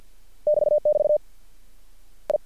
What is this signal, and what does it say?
Morse code, heard as a keyed tone, about 50 wpm, 600 Hz. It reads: <BT><BT> I
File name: archived / CRSnet